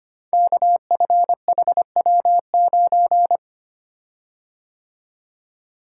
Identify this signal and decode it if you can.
Morse code, heard as a keyed tone, about 25 wpm, 700 Hz. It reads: KFHW9
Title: Code Groups - 5 Letter:Numbers - 25wpm